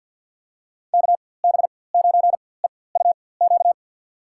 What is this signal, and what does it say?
Morse code, heard as a keyed tone, about 53 wpm, spaced as for 24 wpm, 700 Hz. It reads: KB9EUQ
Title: USA Call Signs - 4x Character Spacing - 50wpm